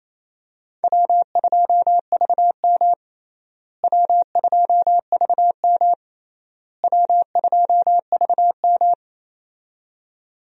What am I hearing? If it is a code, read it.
Morse code, heard as a keyed tone, about 28 wpm, 700 Hz. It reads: W2VM W2VM W2VM
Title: USA Call Signs - 4 Characters - Repeated 3x - 28wpm